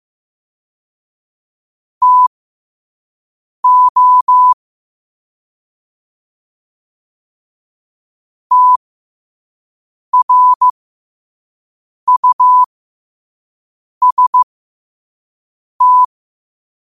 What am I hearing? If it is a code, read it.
Morse code, heard as a keyed tone, about 15 wpm, spaced as for 5 wpm, 1000 Hz. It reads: TO TRUST
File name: qotdfiles / 5wpm